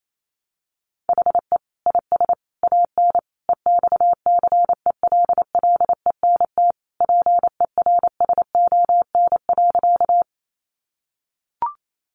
Morse code, heard as a keyed tone, about 28 wpm, 700 Hz. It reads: HE IS AN EXCELLENT PERSON.
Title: Sentences from Top 1000 Words 28wpm